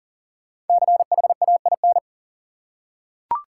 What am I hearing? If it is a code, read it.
Morse code, heard as a keyed tone, about 40 wpm, 700 Hz. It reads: CHAIN
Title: Five Letter Words - 40wpm